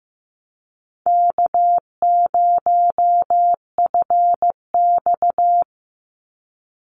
Morse code, heard as a keyed tone, about 15 wpm, 700 Hz. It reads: K0FX
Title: Easy Call Signs Repeated 3x - 15wpm